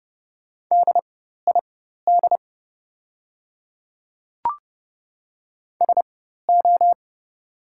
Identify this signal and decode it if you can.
Morse code, heard as a keyed tone, about 30 wpm, spaced as for 14 wpm, 700 Hz. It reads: DID SO
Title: 4x Character Spacing - Top 100 Words - 30wpm